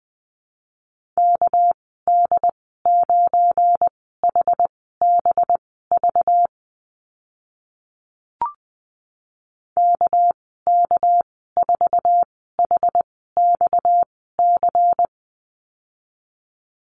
Morse code, heard as a keyed tone, about 20 wpm, spaced as for 14 wpm, 700 Hz. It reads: KD9HBV KK4HXC
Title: 231 Lesson 37g - Letter x - Call Signs 20wpm